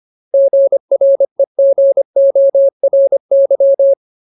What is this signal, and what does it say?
Morse code, heard as a keyed tone, about 25 wpm, 550 Hz. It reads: GREGORY
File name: Ultimate Word List / 25WPM Ultimate